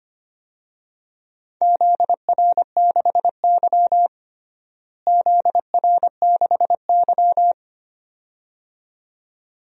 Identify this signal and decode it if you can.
Morse code, heard as a keyed tone, about 25 wpm, 700 Hz. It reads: ZR6Y ZR6Y
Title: DXCC-100 Call Signs - 4 Characters - Repeated 2x - 25wpm